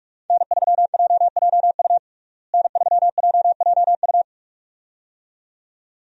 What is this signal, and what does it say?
Morse code, heard as a keyed tone, about 47 wpm, 700 Hz. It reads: N3JJU N3JJU
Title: USA Call Signs - 5 Characters - Repeated 2x - 45wpm